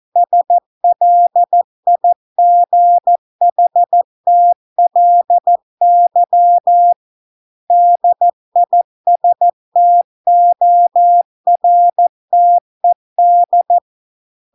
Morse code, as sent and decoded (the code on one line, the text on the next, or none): ... .-.. .. --. .... - .-.. -.-- / -.. .. ... - --- .-. - . -..
SLIGHTLY DISTORTED